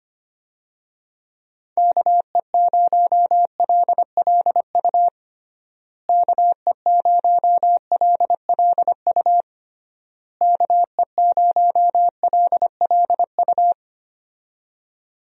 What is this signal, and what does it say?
Morse code, heard as a keyed tone, about 25 wpm, 700 Hz. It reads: KE0LLU KE0LLU KE0LLU
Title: USA Call Signs - 6 Characters - Repeated 3x - 25wpm